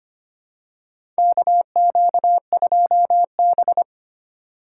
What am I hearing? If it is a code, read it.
Morse code, heard as a keyed tone, about 25 wpm, 700 Hz. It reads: KQ2B
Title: USA Call Signs - 4 Characters - 3x Character Spacing - 25wpm